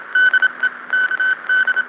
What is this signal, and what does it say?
Morse code, heard as a keyed tone, about 25 wpm, 1500 Hz. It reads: DE KD